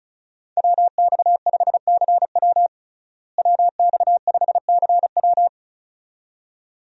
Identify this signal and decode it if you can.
Morse code, heard as a keyed tone, about 35 wpm, 700 Hz. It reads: WX5CW WX5CW
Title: USA Call Signs - 5 Characters - Repeated 2x - 35wpm